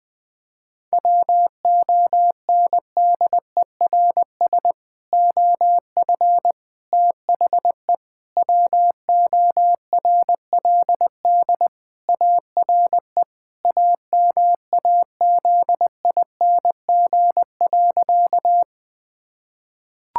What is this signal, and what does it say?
Morse code, heard as a keyed tone, about 20 wpm, 700 Hz. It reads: WONDERS OF THE WORLD ARE AMAZING.
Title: Sentences with 3-Word Phrases - Spoken Beginning - 20wpm